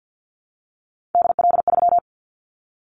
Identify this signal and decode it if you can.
Morse code, heard as a keyed tone, about 50 wpm, 700 Hz. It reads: DL3
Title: Callsign Prefixes for Top 25 DXCC - 50wpm